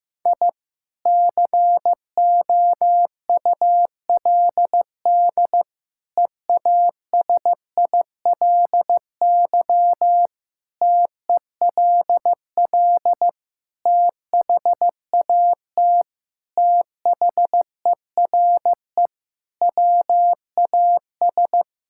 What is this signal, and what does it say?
Morse code, heard as a keyed tone, about 16 wpm, 700 Hz. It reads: I COULD EASILY TELL THAT THERE WAS
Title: Sentences from Top 1000 Words - Review Part 6 - 15wpm